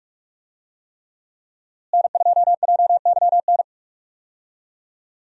Morse code, heard as a keyed tone, about 45 wpm, 700 Hz. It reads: N2JYN
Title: USA Call Signs - 5 Characters - Repeated 3x - 45wpm